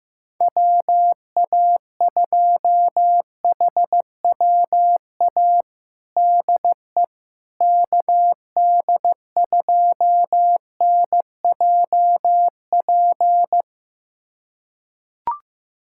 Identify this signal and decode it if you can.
Morse code, heard as a keyed tone, about 15 wpm, 700 Hz. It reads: WA2HWA DE KD2NJP
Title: USA Callsign DE Callsign - 6 Characters - 15wpm